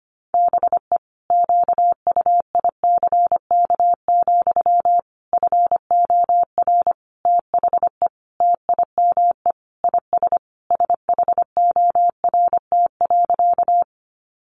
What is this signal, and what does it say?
Morse code, heard as a keyed tone, about 25 wpm, 700 Hz. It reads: BE QUICK, FOR THE TIME IS SHORT.
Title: Sentences from Top 500 Words 25wpm